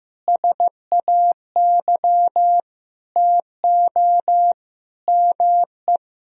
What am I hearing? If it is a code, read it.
Morse code, heard as a keyed tone, about 15 wpm, 700 Hz. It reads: SAY TO ME
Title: Sentences from Top 1000 Words - Review Part 3 - 15wpm